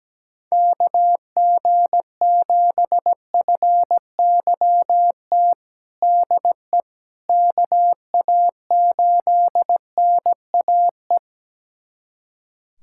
Morse code, heard as a keyed tone, about 17 wpm, 700 Hz. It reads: KG7FYT DE KA8NAE
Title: USA Callsign DE Callsign - 6 Characters - 17wpm